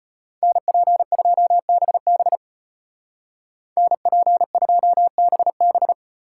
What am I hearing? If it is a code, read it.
Morse code, heard as a keyed tone, about 33 wpm, 700 Hz. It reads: NP2BB NP2BB